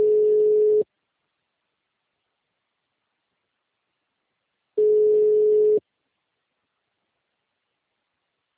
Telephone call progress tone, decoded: ringback tone

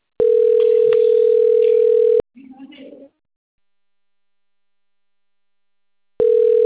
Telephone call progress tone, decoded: ringback tone

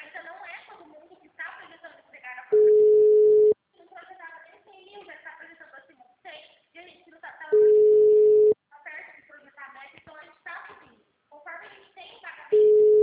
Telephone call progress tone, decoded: ringback tone

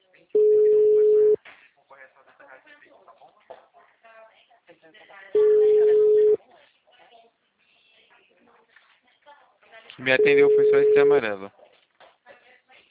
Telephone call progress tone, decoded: ringback tone

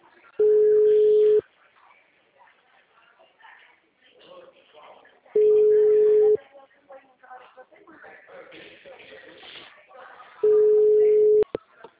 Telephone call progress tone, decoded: ringback tone

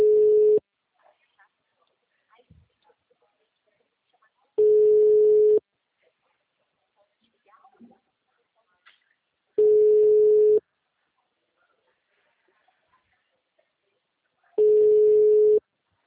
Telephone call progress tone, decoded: ringback tone

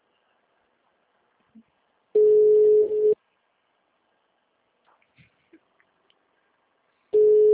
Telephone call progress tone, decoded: ringback tone

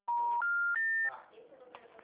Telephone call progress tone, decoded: special information tone

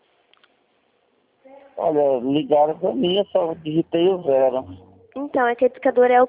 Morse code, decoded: none

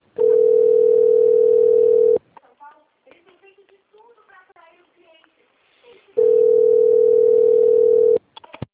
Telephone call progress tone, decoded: ringback tone

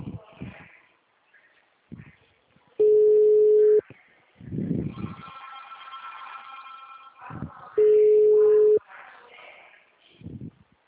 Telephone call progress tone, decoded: ringback tone

